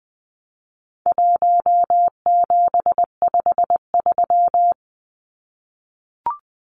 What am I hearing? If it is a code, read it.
Morse code, heard as a keyed tone, about 20 wpm, 700 Hz. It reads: 1753
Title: Numbers - 4-Digit 20wpm